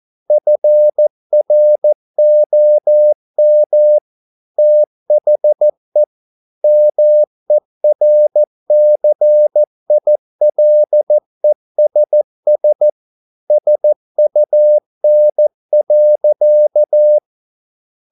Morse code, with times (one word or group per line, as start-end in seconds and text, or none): 0.3-4.0 FROM
4.6-6.0 THE
6.6-12.9 MERCILESS
13.5-17.2 SUN.